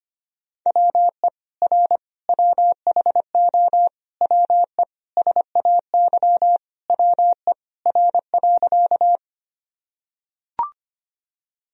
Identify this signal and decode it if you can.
Morse code, heard as a keyed tone, about 25 wpm, 700 Hz. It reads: WE R WHO WE SAY WE R.
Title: Sentences from Top 200 Words and Abbreviations 25wpm